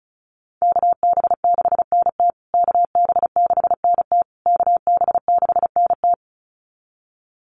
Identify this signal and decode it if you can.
Morse code, heard as a keyed tone, about 34 wpm, 700 Hz. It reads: KB6NT KB6NT KB6NT
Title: USA Call Signs Repeated 3x 35wpm